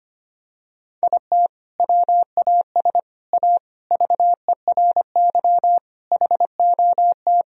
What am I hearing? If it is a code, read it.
Morse code, heard as a keyed tone, about 25 wpm, 700 Hz. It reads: IT WAS A VERY HOT
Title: Sentences from Top 1000 Words - Review Part 3 - 25wpm